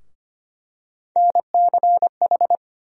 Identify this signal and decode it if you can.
Morse code, heard as a keyed tone, about 25 wpm, 700 Hz. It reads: NCH
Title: USA Call Sign Postfixes - 3 Letters 25wpm